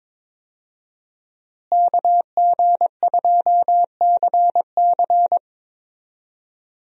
Morse code, heard as a keyed tone, about 22 wpm, 700 Hz. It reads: KG2CC
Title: USA Call Signs - 5 Characters - Repeated 2x - 22wpm